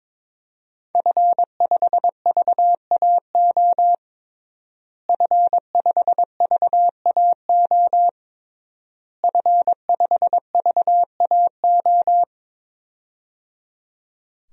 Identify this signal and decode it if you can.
Morse code, heard as a keyed tone, about 21 wpm, 700 Hz. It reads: F5VAO F5VAO F5VAO
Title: DXCC-100 Call Signs - 5 Characters - Repeated 3x - 22wpm